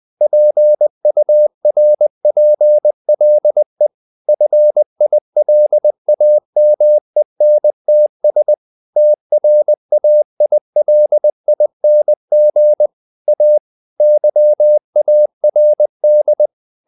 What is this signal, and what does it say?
Morse code, heard as a keyed tone, about 20 wpm, 600 Hz. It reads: PURPLE FILAMENTS TRAILING A YARD
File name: the old man and the sea / the old man and the sea part 1 20wpm